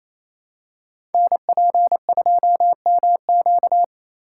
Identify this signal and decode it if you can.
Morse code, heard as a keyed tone, about 29 wpm, 700 Hz. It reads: NP2MQ
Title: Code Groups - 5 Letter:Numbers - 28wpm